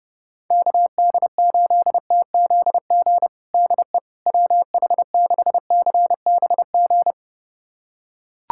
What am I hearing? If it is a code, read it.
Morse code, heard as a keyed tone, about 30 wpm, 690 Hz. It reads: KD8TZG DE WH6CBG
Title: USA Callsign DE Callsign - 6 Characters - 30wpm